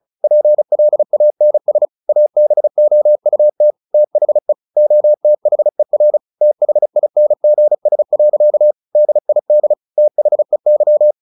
Morse code, heard as a keyed tone, about 34 wpm, 600 Hz. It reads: PLANS ABOUT THE OTHER THINGS. DID THEY
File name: the old man and the sea 2 35wpm